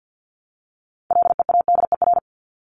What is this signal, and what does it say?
Morse code, heard as a keyed tone, about 50 wpm, 700 Hz. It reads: LEADER
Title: Top 1501-1750 Words 50wpm